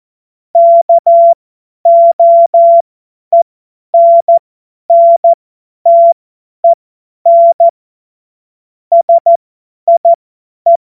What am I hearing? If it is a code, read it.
Morse code, heard as a keyed tone, about 14 wpm, spaced as for 10 wpm, 680 Hz. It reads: KOENNTEN SIE
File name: Morsekurs / Witze / Witz 11. Juli 2023 50 F70